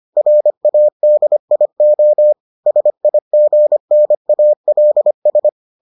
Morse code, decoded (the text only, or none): RADIO SIGNALS